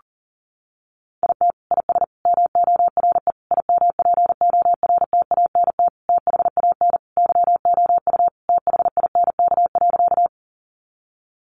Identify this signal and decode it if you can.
Morse code, heard as a keyed tone, about 40 wpm, 700 Hz. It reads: IT IS MORE IMPORTANT THAN YOU THINK.
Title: Sentences from Top 600 Words 40wpm